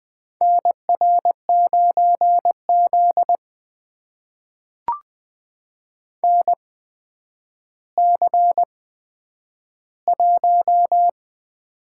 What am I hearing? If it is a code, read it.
Morse code, heard as a keyed tone, about 20 wpm, spaced as for 10 wpm, 700 Hz. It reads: NR9Z NC1
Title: USA Call Signs - 4 Characters - 8x Character Spacing - 20wpm